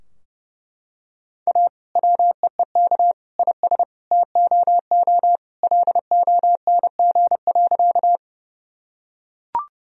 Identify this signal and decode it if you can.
Morse code, heard as a keyed tone, about 32 wpm, 700 Hz. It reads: A WEEK IS TOO LONG.
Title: Sentences from Top 1000 Words - Review Part 2 - 30wpm